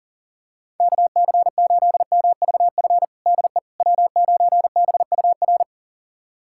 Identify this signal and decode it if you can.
Morse code, heard as a keyed tone, about 37 wpm, 700 Hz. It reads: KC8MVF DE W9BUR
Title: USA Callsign DE Callsign 40wpm